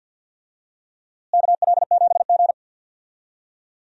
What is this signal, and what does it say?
Morse code, heard as a keyed tone, about 52 wpm, 700 Hz. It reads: KL7G
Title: DXCC-100 Call Signs - 4 Characters - Repeated 3x - 50wpm